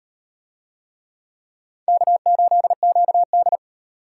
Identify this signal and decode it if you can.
Morse code, heard as a keyed tone, about 38 wpm, 700 Hz. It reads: K8QD